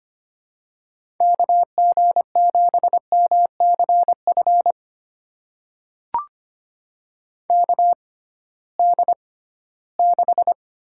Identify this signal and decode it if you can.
Morse code, heard as a keyed tone, about 25 wpm, spaced as for 16 wpm, 700 Hz. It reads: KG7MCF KD6
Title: USA Call Signs - 6x Character Spacing - 25wpm